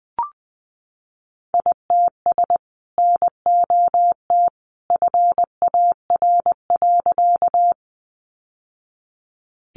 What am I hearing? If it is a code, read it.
Morse code, heard as a keyed tone, about 20 wpm, 700 Hz. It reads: ITS NOT FAR.
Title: Sentences from Top 600 Words 20wpm